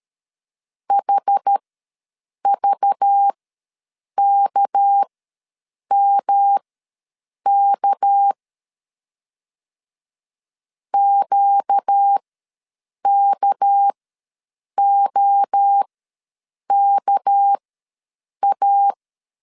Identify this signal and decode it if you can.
Morse code, heard as a keyed tone, about 13 wpm, spaced as for 7 wpm, 800 Hz. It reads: HVKMK QKOKA
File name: LEZIONI / 6 PKX0 / 6 PK ws 12wpm 3116